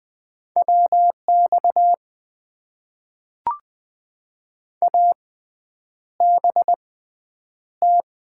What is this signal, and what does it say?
Morse code, heard as a keyed tone, about 20 wpm, spaced as for 9 wpm, 700 Hz. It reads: WX ABT
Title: Top 90 QSO Elements - 6x Character Spacing - 20wpm